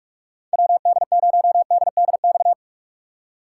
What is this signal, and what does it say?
Morse code, heard as a keyed tone, about 46 wpm, 700 Hz. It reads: WD0DDX
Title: USA Call Signs - 6 Characters - 6x Character Spacing - 45wpm